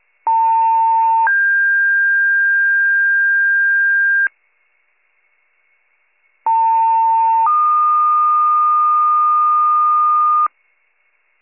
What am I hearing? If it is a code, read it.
Two-tone page, heard as a one-second tone, then a three-second tone, about 900 then 1600 Hz, about 900 then 1220 Hz. It is heard from 0.3 to 10.5 s.